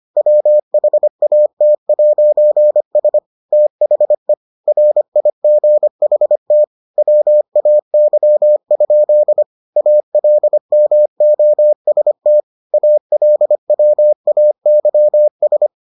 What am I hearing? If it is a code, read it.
Morse code, heard as a keyed tone, about 25 wpm, 600 Hz. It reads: WHAT'S THE RIGHT WAY? ALMOST ALWAYS